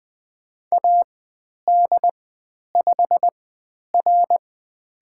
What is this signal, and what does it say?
Morse code, heard as a keyed tone, about 20 wpm, spaced as for 10 wpm, 700 Hz. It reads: AD5R
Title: 095 Lesson 16g - Letter c - Call Signs 20wpm